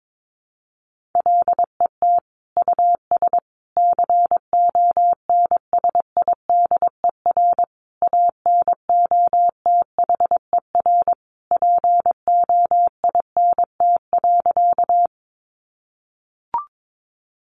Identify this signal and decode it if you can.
Morse code, heard as a keyed tone, about 22 wpm, 700 Hz. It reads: LET US CONSIDER ANOTHER POINT.
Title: Sentences from Top 600 Words 22wpm